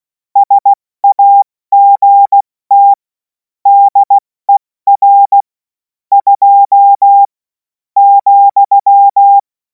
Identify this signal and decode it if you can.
Morse code, heard as a keyed tone, about 16 wpm, 800 Hz. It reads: SAGT DER 2 ,
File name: Morsekurs / Witze / Witz 11. August 2026 70 F80